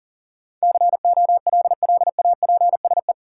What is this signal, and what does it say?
Morse code, heard as a keyed tone, about 40 wpm, 700 Hz. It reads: COLLAPSE
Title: Sets of 5 Words Encore - New Word Drill - 40wpm